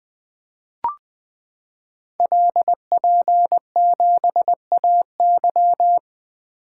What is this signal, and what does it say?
Morse code, heard as a keyed tone, about 20 wpm, 700 Hz. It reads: LP7AY
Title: Code Groups - 5 Letter:Numbers - 20wpm